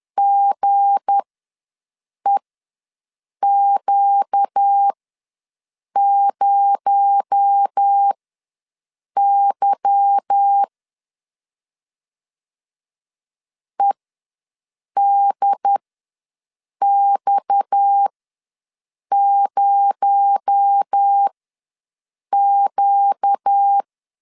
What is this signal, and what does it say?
Morse code, heard as a keyed tone, about 10 wpm, spaced as for 6 wpm, 800 Hz. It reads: GEQ0Y EDX0Q
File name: LEZIONI / 10 SIGNS / 10B INT ws 10wpm 4134